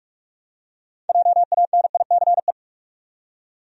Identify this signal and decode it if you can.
Morse code, heard as a keyed tone, about 43 wpm, 700 Hz. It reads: JANICE
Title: USA Common Names - 45wpm